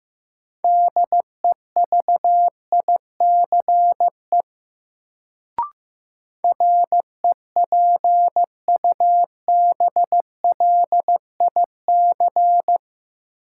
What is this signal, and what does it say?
Morse code, heard as a keyed tone, about 15 wpm, 700 Hz. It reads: DEVICE REPUBLIC
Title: Sentences from Top 1500 Words - New Word Drill - 15wpm